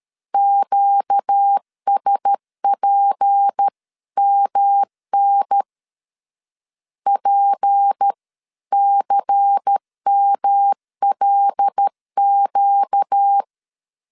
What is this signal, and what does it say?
Morse code, heard as a keyed tone, about 13 wpm, 800 Hz. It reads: QSPMN PCMLQ
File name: LEZIONI / 6 PKX0 / 6 P 12wpm 2033